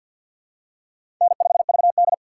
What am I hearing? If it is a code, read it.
Morse code, heard as a keyed tone, about 53 wpm, 700 Hz. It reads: N5VD
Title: USA Call Signs - 4 Characters - 2x Character Spacing - 50wpm